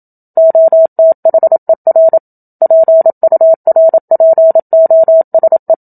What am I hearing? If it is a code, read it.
Morse code, heard as a keyed tone, about 27 wpm, 650 Hz. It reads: OTHER PURPOSE
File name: Book Two / 01-Chapter 1 - 27 WPM